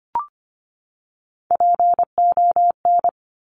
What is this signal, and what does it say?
Morse code, heard as a keyed tone, about 25 wpm, 700 Hz. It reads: PON
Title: Top Three Letter Combinations in Words 25wpm